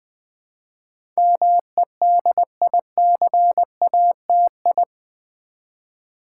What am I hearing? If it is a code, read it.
Morse code, heard as a keyed tone, about 20 wpm, 700 Hz. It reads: MEDICATI
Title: Build A Word - 4 Letters At A Time - 10-Letter Words - 20wpm